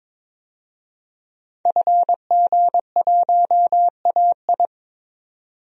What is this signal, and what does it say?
Morse code, heard as a keyed tone, about 22 wpm, 700 Hz. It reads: FG1AI
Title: DXCC-100 Call Signs - 5 Characters - 22wpm